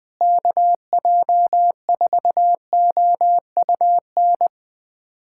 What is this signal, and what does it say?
Morse code, heard as a keyed tone, about 20 wpm, 700 Hz. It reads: KJ4OUN
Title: USA Callsign DE Callsign - 6 Characters - 20wpm